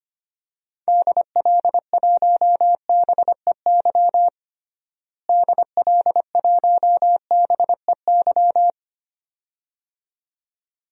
Morse code, heard as a keyed tone, about 25 wpm, 700 Hz. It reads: DL1BEY DL1BEY
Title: DXCC-100 Call Signs - 6 Characters - Repeated 2x - 25wpm